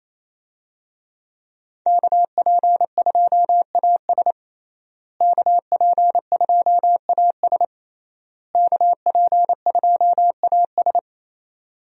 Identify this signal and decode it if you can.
Morse code, heard as a keyed tone, about 27 wpm, 700 Hz. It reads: KP2AS KP2AS KP2AS
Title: USA Call Signs - 5 Characters - Repeated 3x - 28wpm